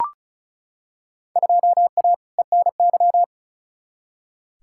Morse code, heard as a keyed tone, about 34 wpm, 700 Hz. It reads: 2A ENY
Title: Field Day Exchanges Encore - 35wpm